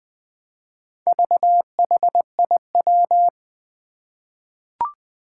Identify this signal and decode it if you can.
Morse code, heard as a keyed tone, about 20 wpm, 700 Hz. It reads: VHIW
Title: Four Challenging Letter-Number Combinations 20wpm